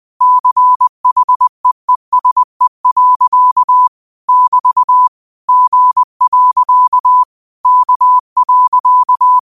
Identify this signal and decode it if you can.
Morse code, heard as a keyed tone, about 20 wpm, 1000 Hz. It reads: CHEESE. <BT> G. K.
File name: qotdfiles / 20wpm